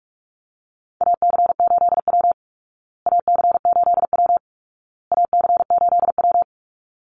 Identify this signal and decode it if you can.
Morse code, heard as a keyed tone, about 44 wpm, 700 Hz. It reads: AC8W AC8W AC8W
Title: Call Signs Repeated 2x - 45wpm